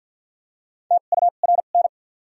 Morse code, heard as a keyed tone, about 47 wpm, spaced as for 35 wpm, 700 Hz. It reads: TURN
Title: 2x Character Spacing - Top 500 Words - 50wpm